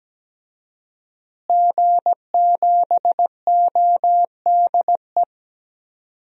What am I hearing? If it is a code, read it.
Morse code, heard as a keyed tone, about 17 wpm, 700 Hz. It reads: G7ODE
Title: DXCC-100 Call Signs - 5 Characters - Repeated 2x - 17wpm